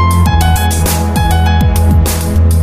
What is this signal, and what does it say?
There is no code.